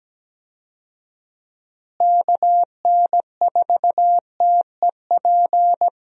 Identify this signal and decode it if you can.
Morse code, heard as a keyed tone, about 17 wpm, 700 Hz. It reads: KN4TEP
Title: USA Call Signs - 6 Characters - Repeated 3x - 17wpm